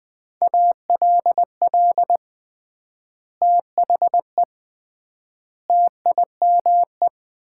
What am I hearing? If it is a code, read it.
Morse code, heard as a keyed tone, about 19 wpm, 700 Hz. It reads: ALL THE TIME
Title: 3x Word-Spacing - Sets of 3 Words - 20wpm